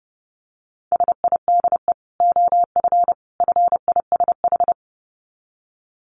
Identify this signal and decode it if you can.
Morse code, heard as a keyed tone, about 30 wpm, 700 Hz. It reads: SIDE OF FISH
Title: Dit Challenge 30wpm